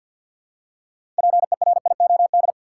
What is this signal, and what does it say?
Morse code, heard as a keyed tone, about 52 wpm, 700 Hz. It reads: PERIOD
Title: Six Letter Words - 50wpm